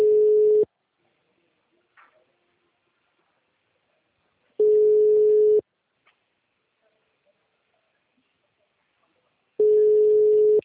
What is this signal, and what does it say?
Ringback tone (Europe, 425 Hz).
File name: out- -2016- - - .4